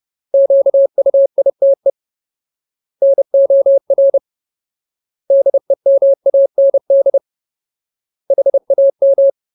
Morse code, heard as a keyed tone, about 30 wpm, 550 Hz. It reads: QUITE NOR DEMAND HAM
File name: Ultimate Word List / 30WPM Ultimate